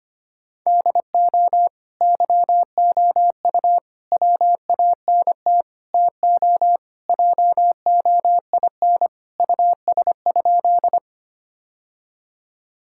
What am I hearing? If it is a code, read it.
Morse code, heard as a keyed tone, about 25 wpm, 700 Hz. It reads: DO YOU WANT TO JOIN US?
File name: practice / 25wpm / Sentences with 4-Word Phrases - 25wpm